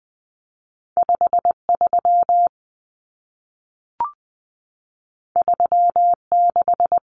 Morse code, heard as a keyed tone, about 20 wpm, 700 Hz. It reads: 53 36
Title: Common Cut Numbers - 2-Digit 20wpm